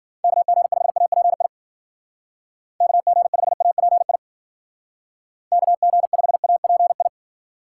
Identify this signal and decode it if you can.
Morse code, heard as a keyed tone, about 52 wpm, 700 Hz. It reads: KG5API KG5API KG5API